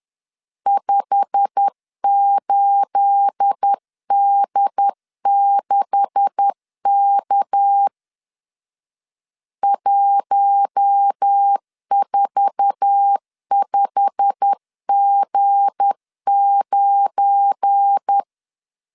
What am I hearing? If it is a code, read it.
Morse code, heard as a keyed tone, about 10 wpm, 800 Hz. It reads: 58D6K 145G9